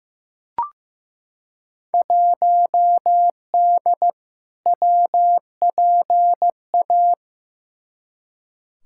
Morse code, heard as a keyed tone, about 15 wpm, 700 Hz. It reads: 1D WPA
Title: Field Day Exchanges - 15wpm